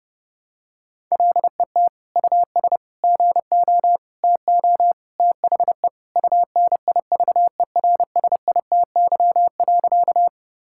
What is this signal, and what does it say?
Morse code, heard as a keyed tone, about 32 wpm, 700 Hz. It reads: LET US GO TO THE UNIVERSITY.
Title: Sentences from Top 1000 Words - Review Part 5 - 30wpm